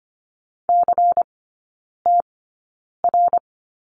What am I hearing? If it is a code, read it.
Morse code, heard as a keyed tone, about 25 wpm, spaced as for 9 wpm, 700 Hz. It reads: CTR
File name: course / 25:10wpm / 093 Lesson 16e - Letter c - Three Characters 25wpm